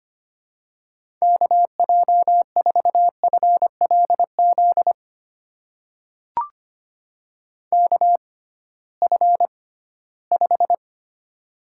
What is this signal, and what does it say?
Morse code, heard as a keyed tone, about 25 wpm, spaced as for 16 wpm, 700 Hz. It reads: KJ4FLZ KF5
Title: USA Call Signs - 6x Character Spacing - 25wpm